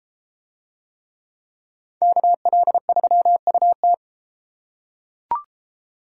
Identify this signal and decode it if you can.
Morse code, heard as a keyed tone, about 32 wpm, 700 Hz. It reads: KL3UT